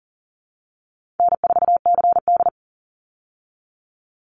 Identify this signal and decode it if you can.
Morse code, heard as a keyed tone, about 40 wpm, 700 Hz. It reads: N4CD
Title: Call Signs Repeated 2x - 40wpm